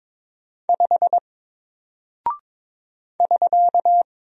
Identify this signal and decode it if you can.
Morse code, heard as a keyed tone, about 22 wpm, 700 Hz. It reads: 5 <SK>
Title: Single Character (Letter:Number:Punctuation:Prosign) Alt 22wpm